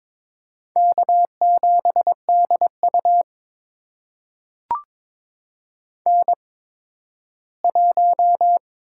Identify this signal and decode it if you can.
Morse code, heard as a keyed tone, about 22 wpm, spaced as for 13 wpm, 700 Hz. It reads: K7DU N1
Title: USA Call Signs - 4 Characters - 8x Character Spacing - 22wpm